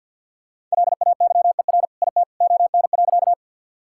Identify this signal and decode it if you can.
Morse code, heard as a keyed tone, about 50 wpm, 700 Hz. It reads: LAYER IT ON.
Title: Sentences from Top 1600 Words 50wpm